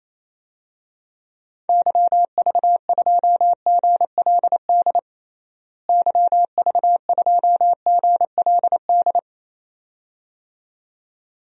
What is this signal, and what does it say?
Morse code, heard as a keyed tone, about 27 wpm, 700 Hz. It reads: YV2GLD YV2GLD
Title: DXCC-100 Call Signs - 6 Characters - Repeated 2x - 28wpm